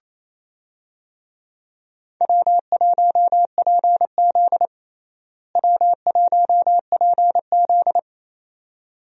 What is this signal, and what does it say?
Morse code, heard as a keyed tone, about 28 wpm, 700 Hz. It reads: W1PZ W1PZ
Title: USA Call Signs - 4 Characters - Repeated 2x - 28wpm